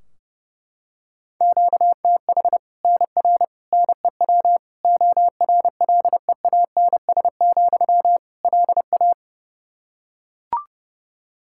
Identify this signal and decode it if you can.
Morse code, heard as a keyed tone, about 30 wpm, 700 Hz. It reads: QTH NR NEW ORLEANS, LA
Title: K1USN SST QSO Practice 30wpm